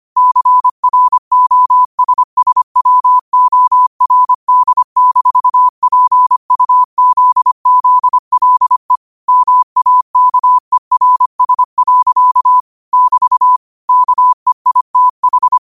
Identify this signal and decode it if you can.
Morse code, heard as a keyed tone, about 25 wpm, 1000 Hz. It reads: CROSSWORD-PUZZLE MAKERS. <BT> KEITH
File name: qotdfiles / 25wpm